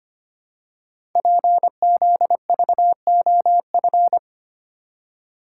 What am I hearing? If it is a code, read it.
Morse code, heard as a keyed tone, about 25 wpm, 700 Hz. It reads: PZVOF
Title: Code Groups - 5 Letters - 25wpm